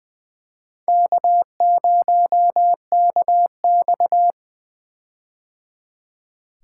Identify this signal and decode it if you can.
Morse code, heard as a keyed tone, about 20 wpm, 700 Hz. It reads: K0KX
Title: K1USN SST QSO Practice - 20wpm